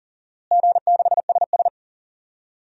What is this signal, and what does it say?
Morse code, heard as a keyed tone, about 40 wpm, 700 Hz. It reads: G6SS
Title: DXCC-100 Call Signs - 4 Characters - Repeated 3x - 40wpm